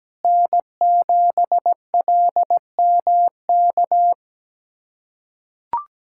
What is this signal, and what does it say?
Morse code, heard as a keyed tone, about 17 wpm, 700 Hz. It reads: N7LMK
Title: USA Callsign DE Callsign - 5 Characters - 17wpm